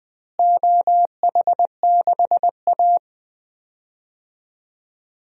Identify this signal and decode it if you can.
Morse code, heard as a keyed tone, about 20 wpm, 700 Hz. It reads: OH6A
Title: DXCC-100 Call Signs - 4 Characters - Repeated 2x - 20wpm